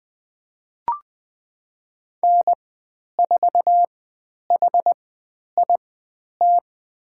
Morse code, heard as a keyed tone, about 20 wpm, spaced as for 10 wpm, 700 Hz. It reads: N4HIT
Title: 095 Lesson 16g - Letter c - Call Signs 20wpm